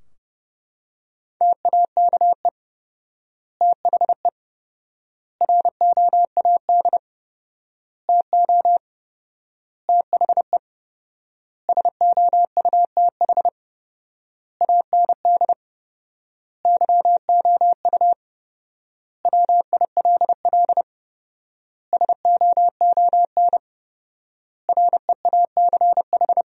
Morse code, heard as a keyed tone, about 30 wpm, 700 Hz. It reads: TAKE THE ROAD TO THE SOUTH AND YOU WILL SOON REACH
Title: 4x Word-Spacing - Sentences from Top 400 Words - 30wpm